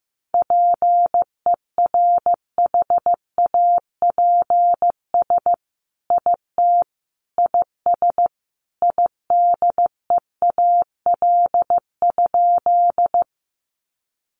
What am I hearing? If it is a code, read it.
Morse code, heard as a keyed tone, about 15 wpm, 700 Hz. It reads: PERHAPS IT IS IDEAL?
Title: Sentences from Top 1400 Words 15wpm